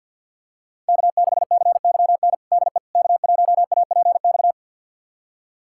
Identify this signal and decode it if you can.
Morse code, heard as a keyed tone, about 51 wpm, 700 Hz. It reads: K6CYN DE K1APX
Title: USA Callsign DE Callsign - 5 Characters - 50wpm